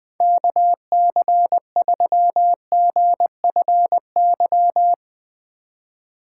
Morse code, heard as a keyed tone, about 20 wpm, 700 Hz. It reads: KC3GFY
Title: Speed Racing - USA 2x3 Call Signs 20wpm